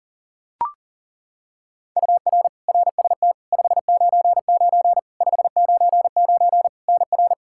Morse code, heard as a keyed tone, about 40 wpm, 700 Hz. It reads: UR RST 599 599 NR
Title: Practice QSO - Exchange 1 - Beginning - 40wpm